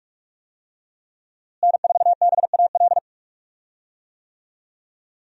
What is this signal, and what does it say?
Morse code, heard as a keyed tone, about 46 wpm, 700 Hz. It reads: N4BAL
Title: USA Call Signs - 5 Characters - Repeated 3x - 45wpm